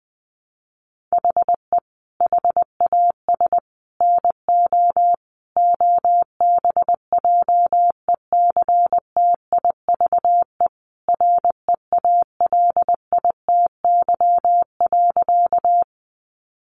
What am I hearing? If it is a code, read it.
Morse code, heard as a keyed tone, about 20 wpm, 700 Hz. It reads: HE HAS NO OBJECTIVE REALITY.
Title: Sentences from Top 1200 Words 20wpm